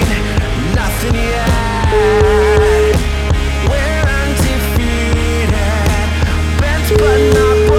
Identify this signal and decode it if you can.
Ringback tone (Europe, 425 Hz), with noise.